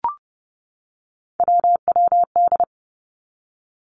Morse code, heard as a keyed tone, about 30 wpm, 700 Hz. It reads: WWD
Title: Three Challenging Letter-Number Combinations 30wpm